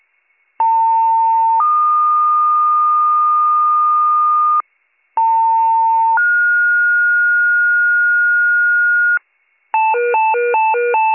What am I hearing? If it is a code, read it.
Two-tone page, heard as a one-second tone, then a three-second tone, about 900 then 1220 Hz, about 900 then 1430 Hz, with noise. It is heard from 0.6 to 9.2 s.